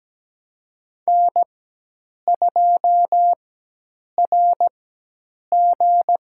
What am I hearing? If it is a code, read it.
Morse code, heard as a keyed tone, about 17 wpm, spaced as for 8 wpm, 700 Hz. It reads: N2RG
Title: USA Call Signs - 4x Character Spacing - 17wpm